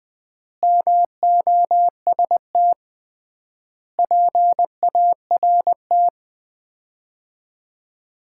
Morse code, heard as a keyed tone, about 21 wpm, 700 Hz. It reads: MOST PART
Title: 3x Word-Spacing - Sets of 4 Words - 20wpm